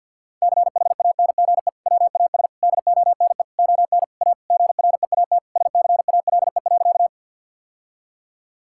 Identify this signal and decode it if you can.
Morse code, heard as a keyed tone, about 50 wpm, 700 Hz. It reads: CHANGE WAS DONE ON A GREAT SCALE.
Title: Sentences from Top 1000 Words - Review Part 5 - 50wpm